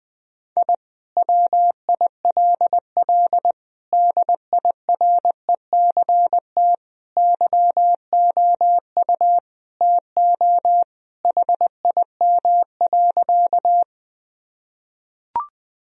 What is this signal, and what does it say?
Morse code, heard as a keyed tone, about 21 wpm, 700 Hz. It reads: I WILL DIRECT YOU TO HIM.
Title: Sentences from Top 1000 Words - Review Part 2 - 20wpm